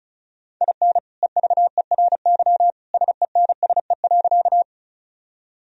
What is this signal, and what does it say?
Morse code, heard as a keyed tone, about 36 wpm, 700 Hz. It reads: IN EVERY SENSE.
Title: Sentences from Top 1000 Words - Review Part 4 - 35wpm